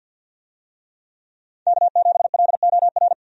Morse code, heard as a keyed tone, about 51 wpm, 700 Hz. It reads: K7LOR
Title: USA Call Signs - 5 Characters - 50wpm